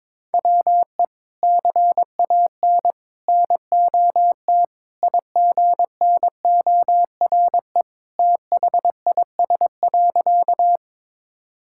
Morse code, heard as a keyed tone, about 23 wpm, 700 Hz. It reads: WE CAN NOT IGNORE THIS.
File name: practice / 22wpm / Sentences with 3-Word Phrases - 22wpm